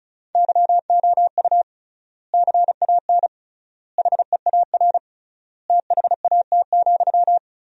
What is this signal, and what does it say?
Morse code, heard as a keyed tone, about 34 wpm, 700 Hz. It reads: YOU CAN HEAR THAT,
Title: 3x Word-Spacing - Sentences from Top 500 Words - 35wpm